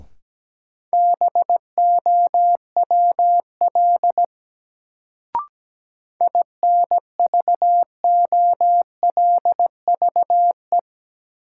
Morse code, heard as a keyed tone, about 17 wpm, 700 Hz. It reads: BOWL INVOLVE
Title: Sentences from Top 1200 Words - New Word Drill - 17wpm